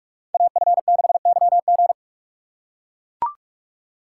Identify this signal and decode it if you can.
Morse code, heard as a keyed tone, about 44 wpm, 690 Hz. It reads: AF6YG
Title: USA Call Signs - 5 Characters - 6x Character Spacing - 45wpm